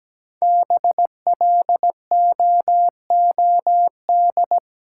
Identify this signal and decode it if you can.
Morse code, heard as a keyed tone, about 17 wpm, 700 Hz. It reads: BLOOD
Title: Sentences from Top 1100 Words - New Word Drill Alt - 17wpm